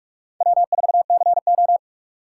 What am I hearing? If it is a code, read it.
Morse code, heard as a keyed tone, about 56 wpm, 700 Hz. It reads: W4CO
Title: Speed Racing - USA 1x2 Call Signs 45wpm